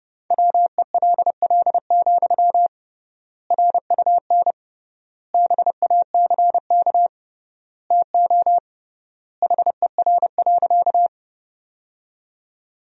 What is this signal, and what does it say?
Morse code, heard as a keyed tone, about 30 wpm, 700 Hz. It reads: WELL, RUN BACK TO HER.
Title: 3x Word-Spacing - Sentences from Top 300 Words - 30wpm